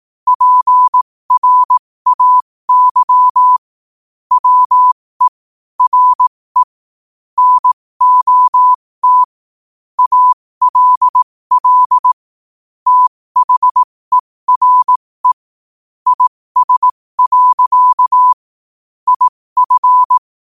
Morse code, heard as a keyed tone, about 18 wpm, spaced as for 15 wpm, 1000 Hz. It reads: PRAY WE RE NOT ALL THERE IS. IF
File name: qotdfiles / 15wpm